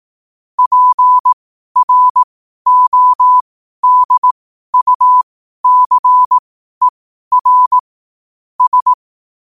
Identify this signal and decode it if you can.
Morse code, heard as a keyed tone, about 18 wpm, spaced as for 12 wpm, 1000 Hz. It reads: PRODUCER S
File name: qotdfiles / 12wpm